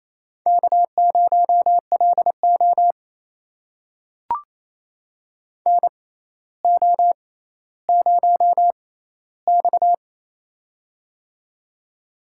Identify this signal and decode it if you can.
Morse code, heard as a keyed tone, about 28 wpm, spaced as for 14 wpm, 700 Hz. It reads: K0LO NO0X
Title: USA Call Signs - 6x Character Spacing - 28wpm